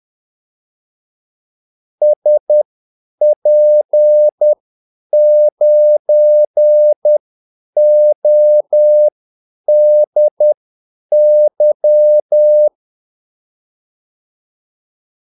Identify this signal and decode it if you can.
Morse code, heard as a keyed tone, about 10 wpm, spaced as for 8 wpm, 600 Hz. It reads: SP9ODY
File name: Callsigns-10-8wpm